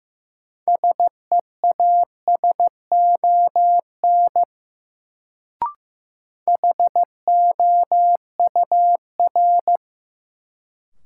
Morse code, heard as a keyed tone, about 15 wpm, 700 Hz. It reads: SEASON HOUR
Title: Sets of 5 Words - New Word Drill - 15wpm